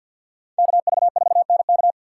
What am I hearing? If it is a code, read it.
Morse code, heard as a keyed tone, about 53 wpm, 700 Hz. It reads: KV4NK
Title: USA Call Signs - 5 Characters - 4x Character Spacing - 50wpm